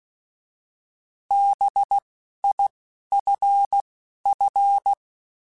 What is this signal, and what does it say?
Morse code, heard as a keyed tone, about 16 wpm, 800 Hz. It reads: B I F F